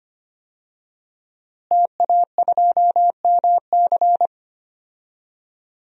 Morse code, heard as a keyed tone, about 25 wpm, 700 Hz. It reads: TA2MC
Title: DXCC-100 Call Signs - 5 Characters - Repeated 3x - 25wpm